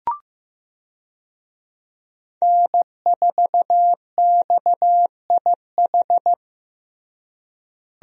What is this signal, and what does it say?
Morse code, heard as a keyed tone, about 15 wpm, 700 Hz. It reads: N4XIH
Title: USA Call Signs - 5 Characters - Repeated 2x - 15wpm